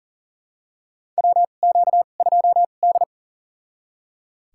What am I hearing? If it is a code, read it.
Morse code, heard as a keyed tone, about 40 wpm, spaced as for 29 wpm, 700 Hz. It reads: WQ2D
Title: USA Call Signs - 2x Character Spacing - 40wpm